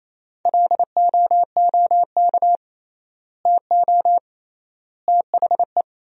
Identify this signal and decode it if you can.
Morse code, heard as a keyed tone, about 28 wpm, 700 Hz. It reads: LOOK TO THE
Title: 3x Word-Spacing - Sentences from Top 400 Words - 28wpm